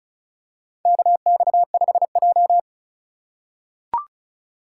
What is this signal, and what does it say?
Morse code, heard as a keyed tone, about 36 wpm, 700 Hz. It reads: KX5J
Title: USA Call Signs - 4x Character Spacing - 35wpm